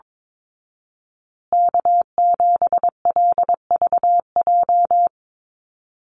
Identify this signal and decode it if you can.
Morse code, heard as a keyed tone, about 22 wpm, 700 Hz. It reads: K7LVJ
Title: Call Signs Repeated 3x - 22wpm